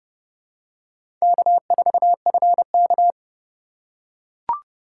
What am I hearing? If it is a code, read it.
Morse code, heard as a keyed tone, about 29 wpm, 700 Hz. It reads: K4FK